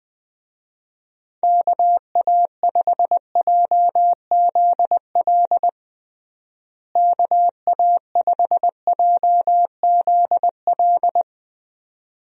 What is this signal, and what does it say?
Morse code, heard as a keyed tone, about 21 wpm, 700 Hz. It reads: KA5JZL KA5JZL
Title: USA Call Signs - 6 Characters - Repeated 2x - 20wpm